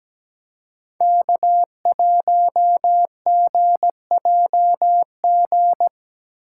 Morse code, heard as a keyed tone, about 17 wpm, 700 Hz. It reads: K1GJG
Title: USA Call Signs - 5 Characters - 4x Character Spacing - 17wpm